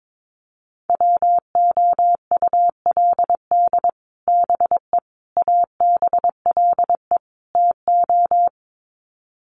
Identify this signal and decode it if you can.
Morse code, heard as a keyed tone, about 22 wpm, 700 Hz. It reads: WOULD BE ABLE TO
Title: Sets of 4 Words 22wpm